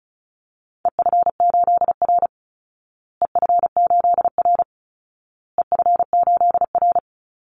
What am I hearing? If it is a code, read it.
Morse code, heard as a keyed tone, about 34 wpm, 700 Hz. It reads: EF8R EF8R EF8R
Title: Call Signs Repeated 2x - 35wpm